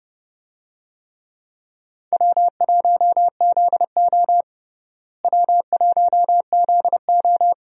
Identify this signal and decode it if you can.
Morse code, heard as a keyed tone, about 30 wpm, 700 Hz. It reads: W1ZO W1ZO
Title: USA Call Signs - 4 Characters - Repeated 2x - 30wpm